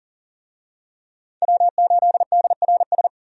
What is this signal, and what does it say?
Morse code, heard as a keyed tone, about 40 wpm, 700 Hz. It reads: W8DRS
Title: USA Call Signs - 5 Characters - 40wpm